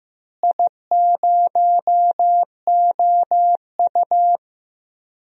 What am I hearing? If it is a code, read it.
Morse code, heard as a keyed tone, about 15 wpm, 700 Hz. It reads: I0OU
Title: DXCC-100 Call Signs - 4 Characters - Repeated 2x - 15wpm